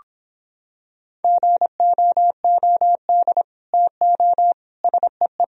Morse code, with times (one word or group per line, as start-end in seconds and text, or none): 1.2-3.4 GOOD
3.7-4.5 TO
4.8-5.4 SEE